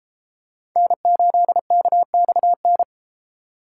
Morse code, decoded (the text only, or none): N8KXN